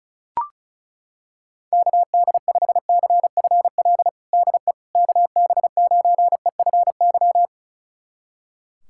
Morse code, heard as a keyed tone, about 34 wpm, 690 Hz. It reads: KD5CFL DE KB9EFY
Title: USA Callsign DE Callsign 35wpm